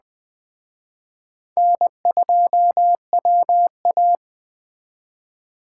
Morse code, heard as a keyed tone, about 20 wpm, 700 Hz. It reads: N2WA